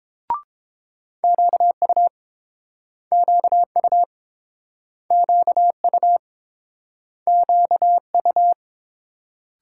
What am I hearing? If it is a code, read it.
Morse code, heard as a keyed tone, about 29 wpm, 700 Hz. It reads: QU QU QU QU